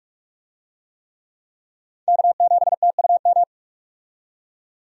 Morse code, heard as a keyed tone, about 46 wpm, 700 Hz. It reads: K7TUM
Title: USA Call Signs - 5 Characters - 45wpm